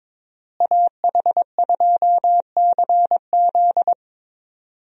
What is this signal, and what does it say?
Morse code, heard as a keyed tone, about 22 wpm, 700 Hz. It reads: AH2CZ
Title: USA Call Signs - 5 Characters - Repeated 3x - 22wpm